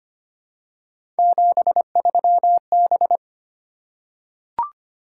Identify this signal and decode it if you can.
Morse code, heard as a keyed tone, about 25 wpm, 700 Hz. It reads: 73B
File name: Three Letter-Number Combinations - 25wpm